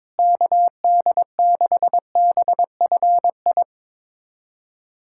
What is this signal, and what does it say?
Morse code, heard as a keyed tone, about 22 wpm, 700 Hz. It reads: KD6BFI
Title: USA Callsign DE Callsign 22wpm